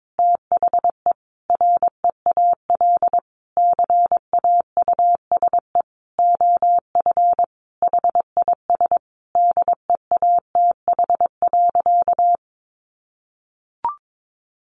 Morse code, heard as a keyed tone, about 22 wpm, 700 Hz. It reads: THE REAL CAUSE OF HIS DEATH.
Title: Sentences from Top 1100 Words 22wpm